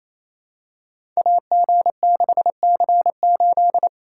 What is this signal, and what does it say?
Morse code, heard as a keyed tone, about 28 wpm, 700 Hz. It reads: AG6C8
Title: Code Groups - 5 Letter:Numbers - 28wpm